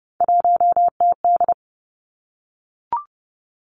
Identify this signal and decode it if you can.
Morse code, heard as a keyed tone, about 32 wpm, 700 Hz. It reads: 1TD
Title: Three Challenging Letter-Number Combinations 30wpm